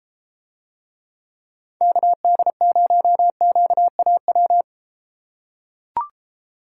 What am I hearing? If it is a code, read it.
Morse code, heard as a keyed tone, about 32 wpm, 700 Hz. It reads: KD0QAW